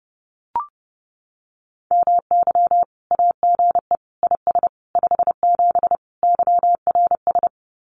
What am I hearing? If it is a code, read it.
Morse code, heard as a keyed tone, about 30 wpm, 700 Hz. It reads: MY AGE IS 57 YRS
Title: Common QSO Phrases Encore 30wpm